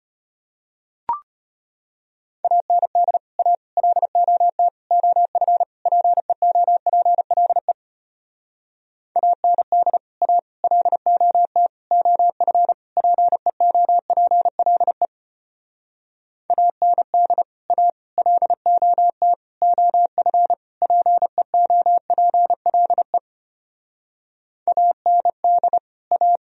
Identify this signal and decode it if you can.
Morse code, heard as a keyed tone, about 33 wpm, 700 Hz. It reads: AND A LOT OF PEOPLE AND A LOT OF PEOPLE AND A LOT OF PEOPLE AND A